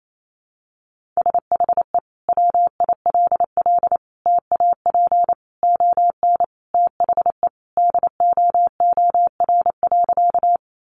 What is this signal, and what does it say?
Morse code, heard as a keyed tone, about 28 wpm, 700 Hz. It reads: SHE WILL TAP ON THE DOOR.
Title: Sentences from Top 1400 Words 28wpm